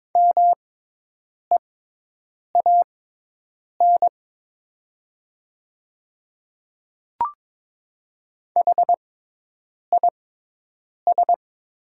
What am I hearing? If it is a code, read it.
Morse code, heard as a keyed tone, about 22 wpm, spaced as for 8 wpm, 700 Hz. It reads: MEAN HIS
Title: 6x Character Spacing - Top 200 Words - 22wpm